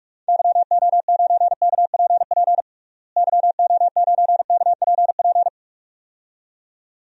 Morse code, heard as a keyed tone, about 41 wpm, 700 Hz. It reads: YO9KPP YO9KPP
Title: DXCC-100 Call Signs - 6 Characters - Repeated 2x - 45wpm